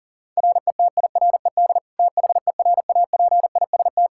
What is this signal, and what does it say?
Morse code, heard as a keyed tone, about 40 wpm, 700 Hz. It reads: RETIRED THERAPIST
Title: Practice QSO - Exchange 3 - More Info - 40wpm